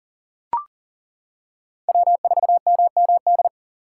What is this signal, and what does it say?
Morse code, heard as a keyed tone, about 37 wpm, 700 Hz. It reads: W4MMD
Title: K1USN SST QSO Practice - 40wpm